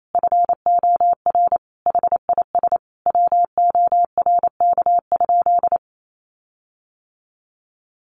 Morse code, heard as a keyed tone, about 28 wpm, 700 Hz. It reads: FOR HIS WORK?
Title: Sentences from Top 1300 Words 28wpm